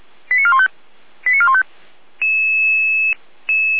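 Five-tone selective call (ZVEI), heard as quick runs of five tones, tones of about 70 ms. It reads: 89415 89415
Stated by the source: ZVEI five-tone selective call received on SDR.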